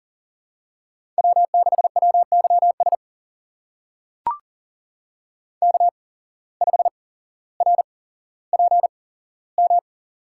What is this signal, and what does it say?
Morse code, heard as a keyed tone, about 40 wpm, spaced as for 17 wpm, 700 Hz. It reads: W6WYS K5RPM
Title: USA Call Signs - 5 Characters - 8x Character Spacing - 40wpm